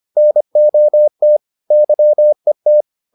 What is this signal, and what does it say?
Morse code, heard as a keyed tone, about 25 wpm, 600 Hz. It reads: NOT YET